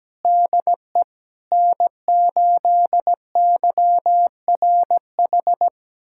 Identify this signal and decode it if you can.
Morse code, heard as a keyed tone, about 17 wpm, 700 Hz. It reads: DE N8YRH
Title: USA Callsign DE Callsign - 5 Characters - 17wpm